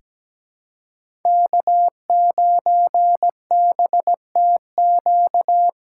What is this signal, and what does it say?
Morse code, heard as a keyed tone, about 17 wpm, 700 Hz. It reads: K9BTQ
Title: K1USN SST QSO Practice - 17wpm